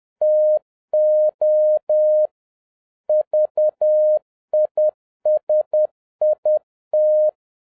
Morse code, heard as a keyed tone, about 10 wpm, 610 Hz. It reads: TO VISIT